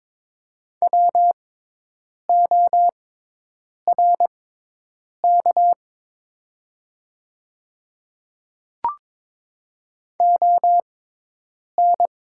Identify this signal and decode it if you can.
Morse code, heard as a keyed tone, about 22 wpm, spaced as for 8 wpm, 700 Hz. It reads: WORK ON
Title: 6x Character Spacing - Top 300 Words - 22wpm